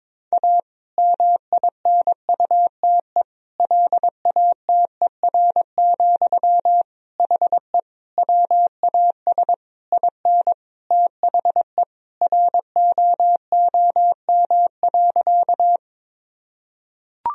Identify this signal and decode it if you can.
Morse code, heard as a keyed tone, about 22 wpm, 700 Hz. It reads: A MINUTE LATER, HE WAS IN THE ROOM.
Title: Sentences from Top 1000 Words - Review Part 4 - 22wpm